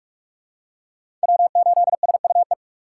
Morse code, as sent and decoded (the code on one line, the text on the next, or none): .-- ---.. ... ..- .
W8SUE